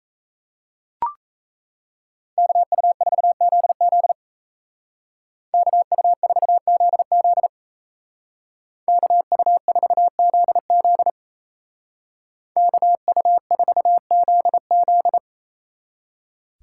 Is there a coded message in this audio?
Yes: Morse code from 2.4 to 15.2 s, heard as a keyed tone, about 36 wpm, 700 Hz. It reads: KU4ZZ KU4ZZ KU4ZZ KU4ZZ